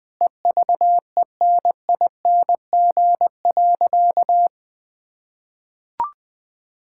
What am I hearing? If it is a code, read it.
Morse code, heard as a keyed tone, about 20 wpm, 700 Hz. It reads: EVENING.
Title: Easy Sentences with Hints - 20wpm